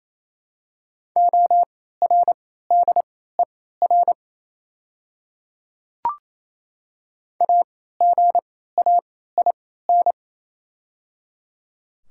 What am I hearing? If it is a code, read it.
Morse code, heard as a keyed tone, about 28 wpm, spaced as for 16 wpm, 700 Hz. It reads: ORDER AGAIN
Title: 3x Character Spacing - Top 500 Words - 28wpm